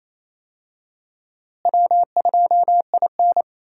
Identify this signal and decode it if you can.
Morse code, heard as a keyed tone, about 27 wpm, 700 Hz. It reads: W2IN
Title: USA Call Signs - 4 Characters - Repeated 3x - 28wpm